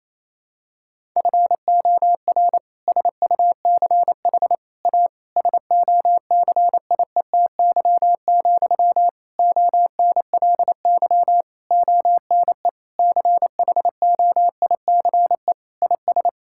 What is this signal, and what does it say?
Morse code, heard as a keyed tone, about 28 wpm, 700 Hz. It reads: FOR SUCH A SOCIETY, ONLY ONE CHOICE IS
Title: Sentences from Top 1000 Words - Review Part 6 - 28wpm